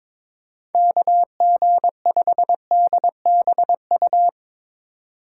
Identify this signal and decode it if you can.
Morse code, heard as a keyed tone, about 22 wpm, 700 Hz. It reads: KG5DBU
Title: USA Call Signs - 6 Characters - 4x Character Spacing - 22wpm